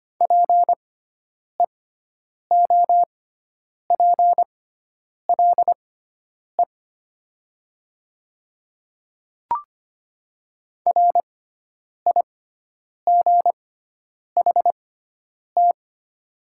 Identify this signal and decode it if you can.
Morse code, heard as a keyed tone, about 25 wpm, spaced as for 9 wpm, 690 Hz. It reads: PEOPLE RIGHT
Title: 6x Character Spacing - Top 200 Words - 25wpm